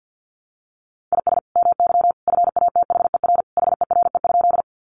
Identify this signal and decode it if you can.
Morse code, heard as a keyed tone, about 50 wpm, 700 Hz. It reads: IS MY FATHER HERE?
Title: Sentences from Top 300 Words 50wpm